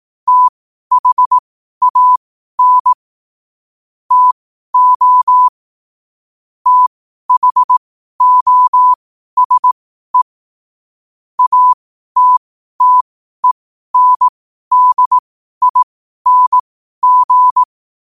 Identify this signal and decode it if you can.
Morse code, heard as a keyed tone, about 18 wpm, spaced as for 12 wpm, 1000 Hz. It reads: THAN TO THOSE ATTENDING
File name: qotdfiles / 12wpm